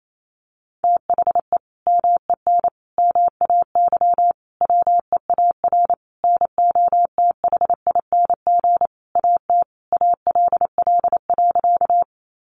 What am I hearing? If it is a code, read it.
Morse code, heard as a keyed tone, about 28 wpm, 700 Hz. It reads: THE MEN MAY WEAR NOTHING AT ALL.
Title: Sentences from Top 1500 Words 28wpm